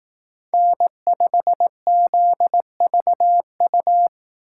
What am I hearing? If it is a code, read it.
Morse code, heard as a keyed tone, about 18 wpm, 700 Hz. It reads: N5ZVU
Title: Speed Racing - USA 1x3 Call Signs 15wpm